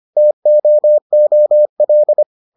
Morse code, heard as a keyed tone, about 25 wpm, 600 Hz. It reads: TOOL